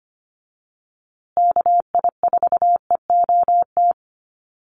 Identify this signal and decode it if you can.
Morse code, heard as a keyed tone, about 25 wpm, 700 Hz. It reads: KI4EOT
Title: USA Call Signs Repeated 3x 25wpm